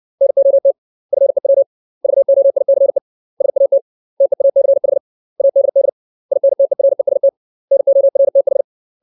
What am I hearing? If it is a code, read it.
Morse code, heard as a keyed tone, about 60 wpm, 550 Hz. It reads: NOT FEW VOICE SAT TEACH ADD INTEREST NORTH
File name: 632 words / 632 Words 60WPM